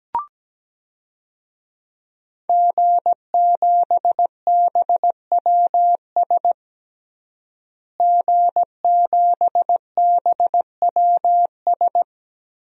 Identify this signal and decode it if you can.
Morse code, heard as a keyed tone, about 17 wpm, 700 Hz. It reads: G7BWS G7BWS
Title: DXCC-100 Call Signs - 5 Characters - Repeated 3x - 17wpm